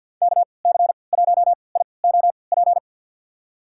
Morse code, heard as a keyed tone, about 53 wpm, spaced as for 29 wpm, 700 Hz. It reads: KC1IOP
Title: USA Call Signs - 6 Characters - 3x Character Spacing - 50wpm